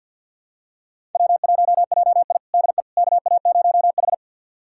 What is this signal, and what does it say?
Morse code, heard as a keyed tone, about 53 wpm, 690 Hz. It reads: W1JI DE KA0H
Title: USA Callsign DE Callsign - 4 Characters - 50wpm